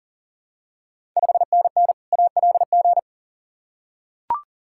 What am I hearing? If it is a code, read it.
Morse code, heard as a keyed tone, about 40 wpm, 700 Hz. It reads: 5NN ALG
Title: Ontario QSO Party Exchange 40wpm